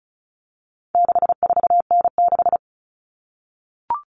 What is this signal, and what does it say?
Morse code, heard as a keyed tone, about 34 wpm, 700 Hz. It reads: 64N6
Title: All Cut Numbers - 4-Digit 35wpm